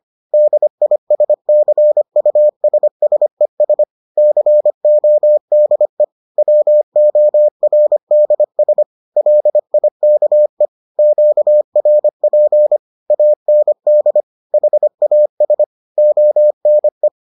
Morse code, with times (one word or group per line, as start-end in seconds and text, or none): 0.3-3.8 DISCUSSES
4.2-6.0 CODE
6.4-8.8 WORDS
9.2-10.7 LIKE
11.0-12.8 QRP
13.1-14.2 AND
14.5-15.6 HAS
16.0-17.1 ONE